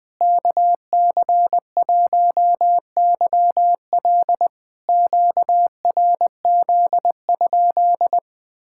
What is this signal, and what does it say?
Morse code, heard as a keyed tone, about 20 wpm, 690 Hz. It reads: KC1YL QRZ?
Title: POTA QSO Practice - 20wpm